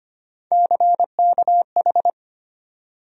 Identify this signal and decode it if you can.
Morse code, heard as a keyed tone, about 25 wpm, 700 Hz. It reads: CKH
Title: USA Call Sign Postfixes - 3 Letters 25wpm